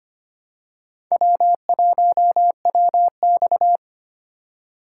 Morse code, heard as a keyed tone, about 25 wpm, 700 Hz. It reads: W1WX
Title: USA Call Signs - 4 Characters - 3x Character Spacing - 25wpm